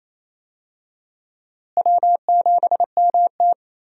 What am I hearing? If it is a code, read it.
Morse code, heard as a keyed tone, about 28 wpm, 700 Hz. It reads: W7MT
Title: USA Call Signs - 4 Characters - Repeated 2x - 28wpm